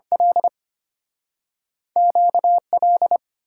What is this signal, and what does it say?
Morse code, heard as a keyed tone, about 33 wpm, 690 Hz. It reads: L QL